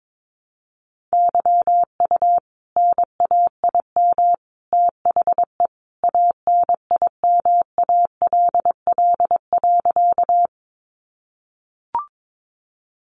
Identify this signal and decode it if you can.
Morse code, heard as a keyed tone, about 22 wpm, 700 Hz. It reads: YU NAIM THE ANIMALL.
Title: A Sound Of Thunder 22wpm